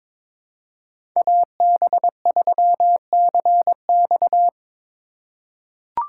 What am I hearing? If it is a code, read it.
Morse code, heard as a keyed tone, about 22 wpm, 700 Hz. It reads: AB3CX
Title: USA Call Signs - 2x Character Spacing - 22wpm